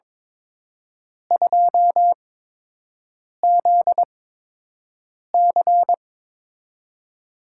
Morse code, heard as a keyed tone, about 23 wpm, spaced as for 6 wpm, 700 Hz. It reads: 2ZC
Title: USA Call Signs - 4 Characters - 8x Character Spacing - 22wpm